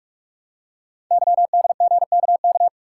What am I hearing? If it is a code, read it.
Morse code, heard as a keyed tone, about 44 wpm, 690 Hz. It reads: YDGKK
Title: Code Groups - 5 Letters - 45wpm